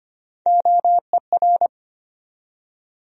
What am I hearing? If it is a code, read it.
Morse code, heard as a keyed tone, about 25 wpm, 700 Hz. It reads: OER
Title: USA Call Sign Postfixes - 3 Letters 25wpm